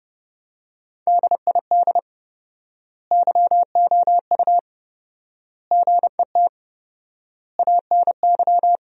Morse code, heard as a keyed tone, about 30 wpm, 700 Hz. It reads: DID YOU GET ANY
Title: 4x Word-Spacing - Sentences from Top 400 Words - 30wpm